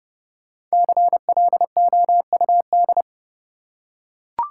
Morse code, heard as a keyed tone, about 32 wpm, 700 Hz. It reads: CLOUD
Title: Five Letter Words - 30wpm